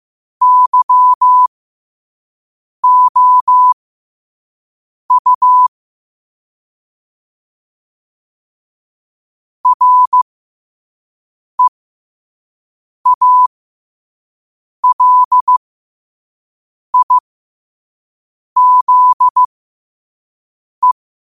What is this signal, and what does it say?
Morse code, heard as a keyed tone, about 15 wpm, spaced as for 5 wpm, 1000 Hz. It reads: YOU REALIZE
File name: qotdfiles / 5wpm